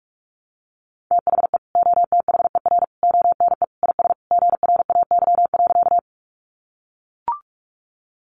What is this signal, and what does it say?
Morse code, heard as a keyed tone, about 44 wpm, 700 Hz. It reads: THE OTHER ONE IS GRAY.
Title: Additional Easy Phrases 45wpm